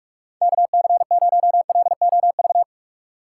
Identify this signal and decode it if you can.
Morse code, heard as a keyed tone, about 46 wpm, 700 Hz. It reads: KC0LOV
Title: USA Call Signs - 6 Characters - 2x Character Spacing - 45wpm